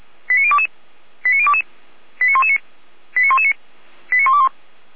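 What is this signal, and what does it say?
Five-tone selective call (ZVEI), heard as quick runs of five tones, tones of about 70 ms. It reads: 89022 89022 89119 89119 89121
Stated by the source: ZVEI five-tone call received by SDR.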